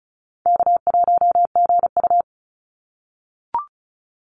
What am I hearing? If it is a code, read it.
Morse code, heard as a keyed tone, about 34 wpm, 700 Hz. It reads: K1GU
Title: Call Signs - 35wpm